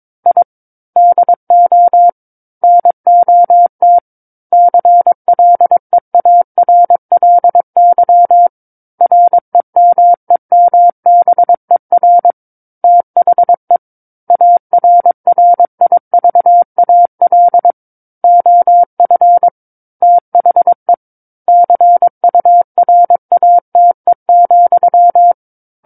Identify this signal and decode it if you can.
Morse code, heard as a keyed tone, about 22 wpm, 700 Hz. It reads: I DO NOT CLEARLY REMEMBER THE ARRIVAL OF THE CURATE,
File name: Book One / 13-Chapter 13 - 22 WPM